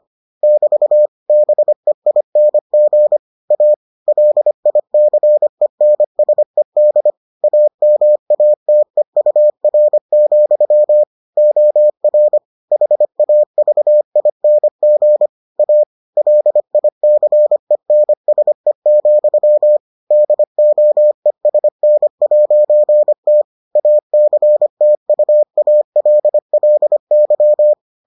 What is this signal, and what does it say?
Morse code, heard as a keyed tone, about 25 wpm, 600 Hz. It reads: <BT> BEING A LICENSED AMATEUR, OR HAVING A LICENSE, DOESN'T ACTUALLY